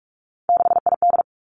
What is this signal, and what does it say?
Morse code, heard as a keyed tone, about 46 wpm, 700 Hz. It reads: 6ID
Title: Three Challenging Letter-Number Combinations 45wpm